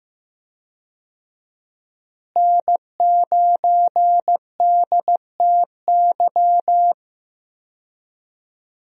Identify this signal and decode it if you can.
Morse code, heard as a keyed tone, about 15 wpm, 700 Hz. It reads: N9DTY
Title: USA Call Signs - 5 Characters - Repeated 3x - 15wpm